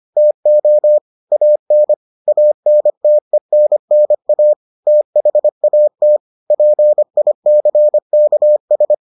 Morse code, heard as a keyed tone, about 25 wpm, 600 Hz. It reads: TO AN ANTENNA THAT PICKS